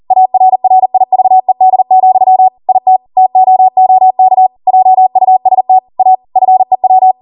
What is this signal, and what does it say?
Morse code, heard as a keyed tone, about 40 wpm, 750 Hz. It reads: ARRIVED, IT TOOK JUST A FEW